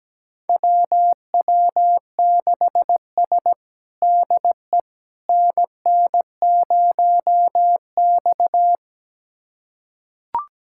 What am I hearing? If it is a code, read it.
Morse code, heard as a keyed tone, about 17 wpm, 700 Hz. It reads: WW6S DE NN0X
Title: USA Callsign DE Callsign - 4 Characters - 17wpm